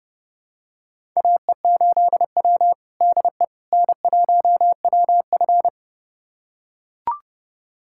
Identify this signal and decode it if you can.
Morse code, heard as a keyed tone, about 30 wpm, 700 Hz. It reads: AE8W DE N1WF
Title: USA Callsign DE Callsign - 4 Characters - 30wpm